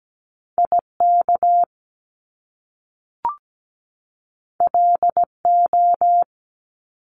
Morse code, heard as a keyed tone, about 17 wpm, 700 Hz. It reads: IK LO
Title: Top Two Letter Combinations in Words 17wpm